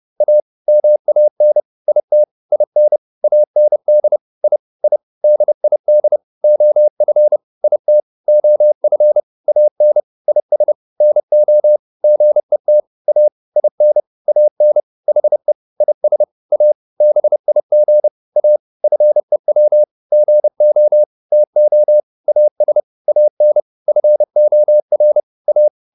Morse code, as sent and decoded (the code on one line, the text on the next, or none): .- / -- .- -. / .. - / .. -. / .- -. -.. / .. / .. / -.. .. -.. / --- ..-. / .. - / --- ..-. / .- -. / .. ... / -. --- / --. - / .- / .. -. / .- -. / .... / .. ... / .- / -... .. --. / .- / ..-. .-- / --. --- / - --- / .- ... / .- -. / ..-. --- .-. / .-
A MAN IT IN AND I I DID OF IT OF AN IS NO GET A IN AN HE IS A BIG A FEW GO TO AS AN FOR A